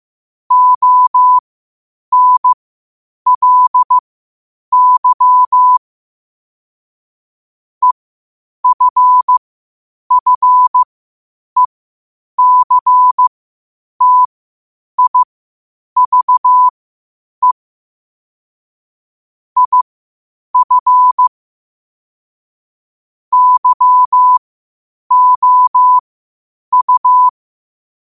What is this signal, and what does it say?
Morse code, heard as a keyed tone, about 15 wpm, spaced as for 8 wpm, 1000 Hz. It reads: ONLY EFFECTIVE IF YOU
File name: qotdfiles / 8wpm